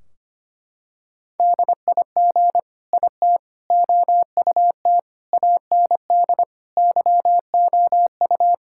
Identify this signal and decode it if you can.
Morse code, heard as a keyed tone, about 25 wpm, 700 Hz. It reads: DIG IT OUT AND YOU
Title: Sentences from Top 1600 Words 25wpm